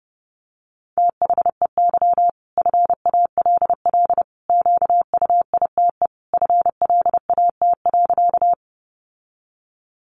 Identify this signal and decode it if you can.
Morse code, heard as a keyed tone, about 30 wpm, 700 Hz. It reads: THEY FALL QUITE FLAT.
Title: Sentences from Top 1000 Words 30wpm